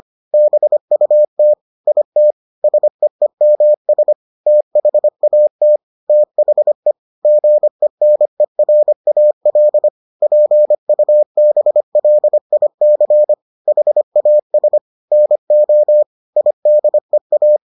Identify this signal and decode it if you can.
Morse code, heard as a keyed tone, about 25 wpm, 600 Hz. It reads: BUT IT SEEMS THAT THE GENERAL PUBLIC HAS NO IDEA